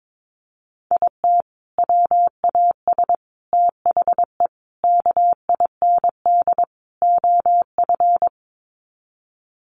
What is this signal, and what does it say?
Morse code, heard as a keyed tone, about 22 wpm, 700 Hz. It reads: IT WAS THE KIND OF
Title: Sets of 5 Words Encore2 22wpm